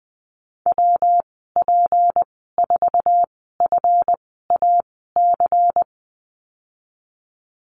Morse code, heard as a keyed tone, about 20 wpm, spaced as for 15 wpm, 700 Hz. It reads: WP4FAC
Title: 173 Lesson 28g - Number 9 - Call Signs 20wpm